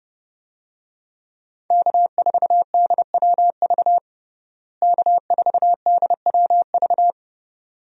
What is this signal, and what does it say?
Morse code, heard as a keyed tone, about 30 wpm, 700 Hz. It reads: K4DWV K4DWV
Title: USA Call Signs - 5 Characters - Repeated 2x - 30wpm